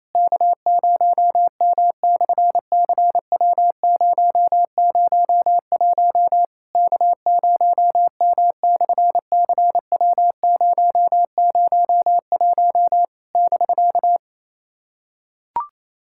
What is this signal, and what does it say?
Morse code, heard as a keyed tone, about 28 wpm, 700 Hz. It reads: K0M/CW001 K0M/CW001 <BK>
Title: SOTA QSO Practice 28wpm